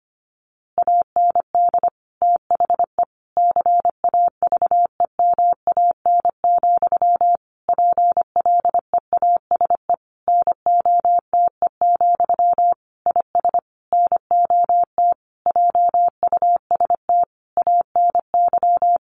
Morse code, heard as a keyed tone, about 25 wpm, 700 Hz. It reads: AND THE CAVEMAN, PLEASE NOTE, IS NOT JUST ANY
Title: A Sound Of Thunder 25wpm